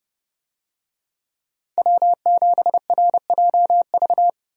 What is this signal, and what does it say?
Morse code, heard as a keyed tone, about 30 wpm, 700 Hz. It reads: W7RJV
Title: USA Call Signs - 5 Characters - Repeated 3x - 30wpm